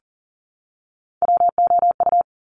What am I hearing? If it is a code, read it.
Morse code, heard as a keyed tone, about 38 wpm, 700 Hz. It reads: WOU
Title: Top Three Letter Combinations in Words 40wpm